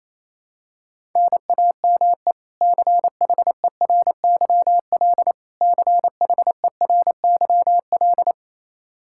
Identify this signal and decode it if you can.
Morse code, heard as a keyed tone, about 28 wpm, 700 Hz. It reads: NAME CHERYL CHERYL
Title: K1USN SST QSO Practice 28wpm